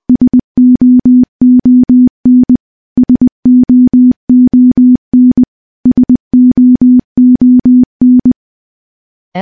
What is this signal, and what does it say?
Morse code, heard as a keyed tone, about 21 wpm, 250 Hz. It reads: SOON SOON SOON
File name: morse-training-20WPM